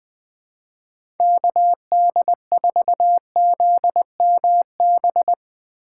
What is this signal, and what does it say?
Morse code, heard as a keyed tone, about 20 wpm, 700 Hz. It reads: KD4ZMB
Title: USA Call Signs - 3x Character Spacing - 20wpm